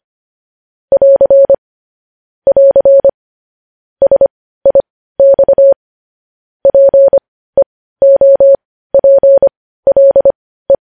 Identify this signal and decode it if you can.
Morse code, heard as a keyed tone, about 25 wpm, spaced as for 15 wpm, 550 Hz. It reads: <AR> <AR> SIX PEOPLE
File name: podcast / News headlines-1x-25x15